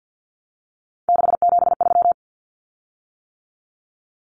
Morse code, heard as a keyed tone, about 50 wpm, 700 Hz. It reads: 673
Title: Common Cut Numbers - 3-Digit 50wpm